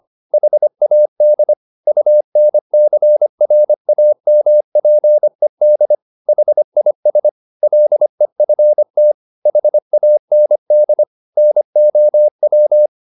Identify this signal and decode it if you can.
Morse code, heard as a keyed tone, about 25 wpm, 600 Hz. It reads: HAD UNCRAMPED HIS LEFT HAND NOW
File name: the old man and the sea part 1 25wpm